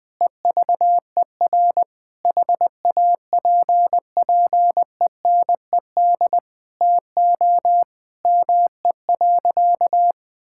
Morse code, heard as a keyed tone, about 21 wpm, 700 Hz. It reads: EVER HAPPENED TO ME.
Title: Sentences with 8-Word Phrases - 20wpm